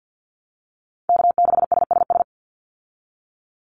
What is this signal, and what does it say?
Morse code, heard as a keyed tone, about 50 wpm, 700 Hz. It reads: K6SSS
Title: Call Signs Repeated 3x - 50wpm